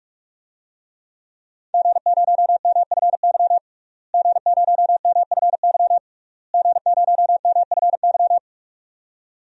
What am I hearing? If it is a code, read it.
Morse code, heard as a keyed tone, about 46 wpm, 700 Hz. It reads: G0MFY G0MFY G0MFY
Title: DXCC-100 Call Signs - 5 Characters - Repeated 3x - 45wpm